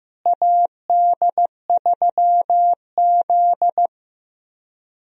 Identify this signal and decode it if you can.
Morse code, heard as a keyed tone, about 16 wpm, 700 Hz. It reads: AD3Z
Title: USA Call Signs - 4 Characters - Repeated 2x - 15wpm